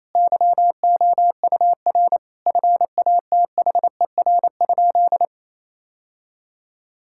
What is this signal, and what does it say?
Morse code, heard as a keyed tone, about 27 wpm, 700 Hz. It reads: YOUR FATHER?
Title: Sentences from Top 1600 Words 28wpm